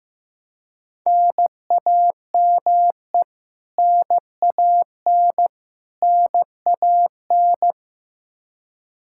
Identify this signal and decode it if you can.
Morse code, heard as a keyed tone, about 15 wpm, 700 Hz. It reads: NAME NAN NAN
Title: K1USN SST QSO Practice 15wpm